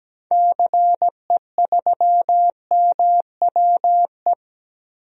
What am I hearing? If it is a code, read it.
Morse code, heard as a keyed tone, about 17 wpm, 700 Hz. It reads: CE3MWE
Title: DXCC-100 Call Signs - 6 Characters - 17wpm